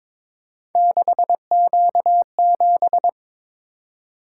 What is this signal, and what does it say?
Morse code, heard as a keyed tone, about 22 wpm, 700 Hz. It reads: 6Q7
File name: practice / 22wpm / Three Letter-Number Combinations - 22wpm